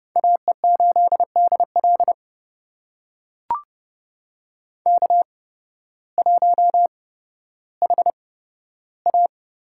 Morse code, heard as a keyed tone, about 29 wpm, spaced as for 14 wpm, 700 Hz. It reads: AE8DL K1HA
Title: USA Call Signs - 8x Character Spacing - 30wpm